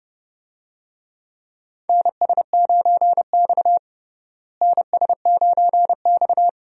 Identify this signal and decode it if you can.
Morse code, heard as a keyed tone, about 30 wpm, 700 Hz. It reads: NS9X NS9X
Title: USA Call Signs - 4 Characters - Repeated 2x - 30wpm